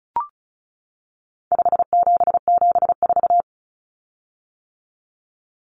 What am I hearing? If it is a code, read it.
Morse code, heard as a keyed tone, about 34 wpm, 700 Hz. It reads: 5774
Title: Common Cut Numbers - 4-Digit 35wpm